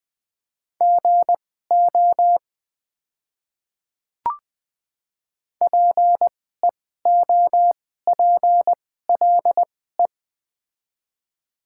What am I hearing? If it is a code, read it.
Morse code, heard as a keyed tone, about 20 wpm, spaced as for 15 wpm, 690 Hz. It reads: GO PEOPLE
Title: 2x Character Spacing - Top 100 Words - 20wpm